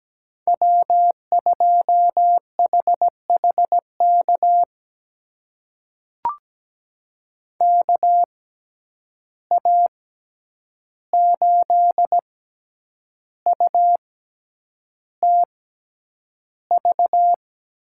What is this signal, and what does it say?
Morse code, heard as a keyed tone, about 17 wpm, spaced as for 8 wpm, 700 Hz. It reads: W2HHK KA8UTV
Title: USA Call Signs - 6x Character Spacing - 17wpm